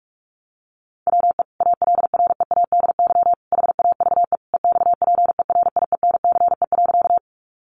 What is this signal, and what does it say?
Morse code, heard as a keyed tone, about 44 wpm, 700 Hz. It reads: WE ALREADY HAVE EXPERIENCE.
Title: Sentences from Top 700 Words 45wpm